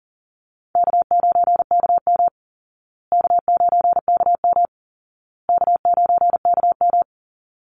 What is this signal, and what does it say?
Morse code, heard as a keyed tone, about 40 wpm, 700 Hz. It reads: K9KM K9KM K9KM
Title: Call Signs Repeated 2x - 40wpm